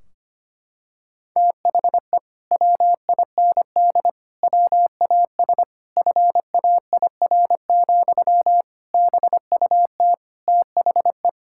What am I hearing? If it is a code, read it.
Morse code, heard as a keyed tone, about 25 wpm, 700 Hz. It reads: THE WIND WAS FAIR, BUT THE
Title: Sentences from Top 1600 Words 25wpm